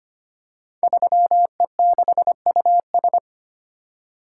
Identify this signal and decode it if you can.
Morse code, heard as a keyed tone, about 25 wpm, 700 Hz. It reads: 3E6US
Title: Code Groups - 5 Letter:Numbers - 25wpm